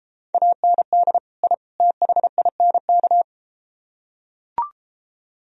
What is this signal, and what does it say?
Morse code, heard as a keyed tone, about 33 wpm, 700 Hz. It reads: AND I THINK